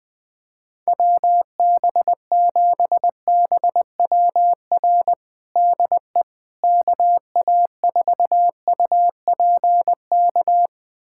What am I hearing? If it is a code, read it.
Morse code, heard as a keyed tone, about 20 wpm, 700 Hz. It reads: WB7BWR DE KA4UPK
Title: USA Callsign DE Callsign 20wpm